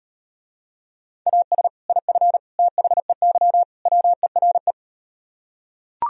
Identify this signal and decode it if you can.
Morse code, heard as a keyed tone, about 39 wpm, 700 Hz. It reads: AS IF THEY WERE